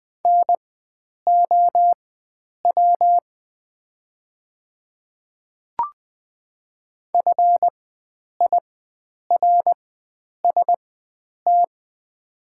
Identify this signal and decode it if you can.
Morse code, heard as a keyed tone, about 20 wpm, spaced as for 9 wpm, 700 Hz. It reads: NOW FIRST
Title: 4x Character Spacing - Top 100 Words - 20wpm